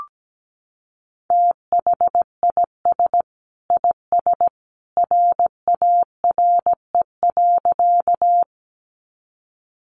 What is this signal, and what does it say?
Morse code, heard as a keyed tone, about 17 wpm, 700 Hz. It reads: THIS IS RARE.
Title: Sentences from Top 1100 Words 17wpm